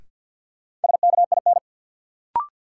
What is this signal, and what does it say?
Morse code, heard as a keyed tone, about 52 wpm, 700 Hz. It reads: SKIN
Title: Sentences from Top 1300 Words - New Word Drill Alt - 50wpm